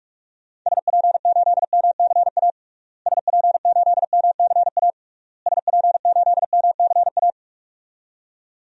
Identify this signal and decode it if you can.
Morse code, heard as a keyed tone, about 41 wpm, 700 Hz. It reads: SP8MCA SP8MCA SP8MCA
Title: DXCC-100 Call Signs - 6 Characters - Repeated 3x - 45wpm